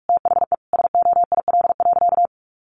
Morse code, heard as a keyed tone, about 44 wpm, 700 Hz. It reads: THE SOIL.
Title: Sentences from Top 1500 Words 45wpm